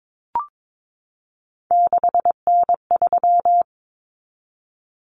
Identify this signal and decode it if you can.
Morse code, heard as a keyed tone, about 22 wpm, 700 Hz. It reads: 6N3
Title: Common Cut Numbers - 3-Digit 22wpm